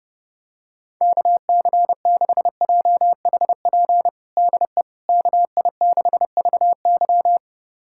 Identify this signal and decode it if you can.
Morse code, heard as a keyed tone, about 30 wpm, 700 Hz. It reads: KC6JHP DE KI6VY
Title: USA Callsign DE Callsign 30wpm